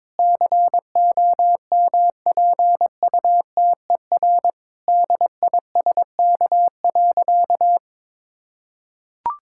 Morse code, heard as a keyed tone, about 22 wpm, 700 Hz. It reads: COMPUTER DISK.
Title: Sentences from Top 1000 Words - Review Part 6 - 22wpm